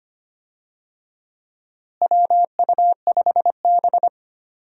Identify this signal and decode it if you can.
Morse code, heard as a keyed tone, about 25 wpm, 700 Hz. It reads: WU5B
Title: USA Call Signs - 4 Characters - Repeated 2x - 25wpm